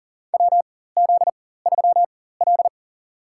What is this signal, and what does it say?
Morse code, heard as a keyed tone, about 40 wpm, spaced as for 19 wpm, 700 Hz. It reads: WZ3L
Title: USA Call Signs - 4 Characters - 4x Character Spacing - 40wpm